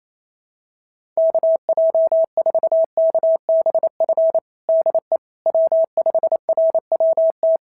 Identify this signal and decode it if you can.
Morse code, heard as a keyed tone, about 28 wpm, 660 Hz. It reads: KJ4KBF DE W5RWT
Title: SOTA QSO Practice 28wpm